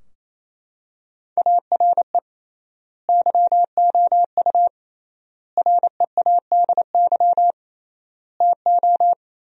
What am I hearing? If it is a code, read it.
Morse code, heard as a keyed tone, about 28 wpm, 700 Hz. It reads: ARE YOU READY TO
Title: 3x Word-Spacing - Sentences from Top 400 Words - 28wpm